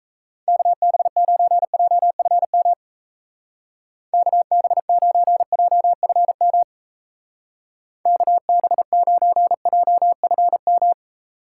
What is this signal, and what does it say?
Morse code, heard as a keyed tone, about 34 wpm, 700 Hz. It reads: KB9JFM KB9JFM KB9JFM
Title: Speed Racing - USA Call Signs 28wpm